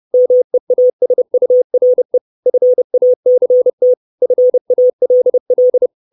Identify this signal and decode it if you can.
Morse code, heard as a keyed tone, about 30 wpm, 500 Hz. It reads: MEASURE FACT FALL
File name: 632 Words 30WPM